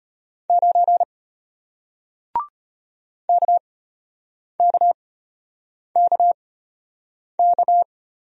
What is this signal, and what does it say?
Morse code, heard as a keyed tone, about 33 wpm, spaced as for 8 wpm, 700 Hz. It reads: 9 KKKK